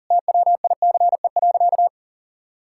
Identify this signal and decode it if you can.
Morse code, heard as a keyed tone, about 40 wpm, 700 Hz. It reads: TWICE.
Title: Sentences from Top 1600 Words 40wpm